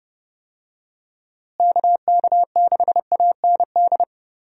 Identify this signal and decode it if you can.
Morse code, heard as a keyed tone, about 32 wpm, 700 Hz. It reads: KK6AND
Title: USA Call Signs - 6 Characters - Repeated 2x - 30wpm